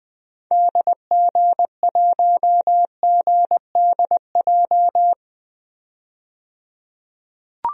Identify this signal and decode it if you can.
Morse code, heard as a keyed tone, about 20 wpm, 700 Hz. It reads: DG1GDJ
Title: DXCC-100 Call Signs - 6 Characters - 20wpm